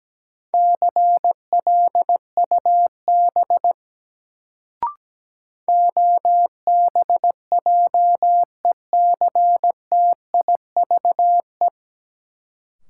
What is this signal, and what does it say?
Morse code, heard as a keyed tone, about 17 wpm, 700 Hz. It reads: CLUB OBJECTIVE
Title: Sentences from Top 1200 Words - New Word Drill Alt - 17wpm